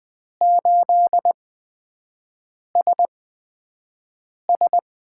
Morse code, heard as a keyed tone, about 20 wpm, spaced as for 5 wpm, 700 Hz. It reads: 8SS
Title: USA Call Signs - 4 Characters - 8x Character Spacing - 20wpm